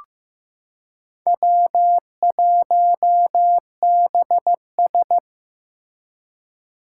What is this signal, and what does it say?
Morse code, heard as a keyed tone, about 15 wpm, 700 Hz. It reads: W1BS
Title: Four Challenging Letter-Number Combinations 15wpm